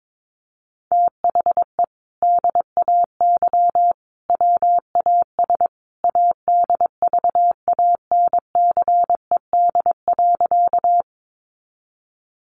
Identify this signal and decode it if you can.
Morse code, heard as a keyed tone, about 22 wpm, 700 Hz. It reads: THE DAY WAS ADVANCED.
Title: Sentences from Top 1200 Words 22wpm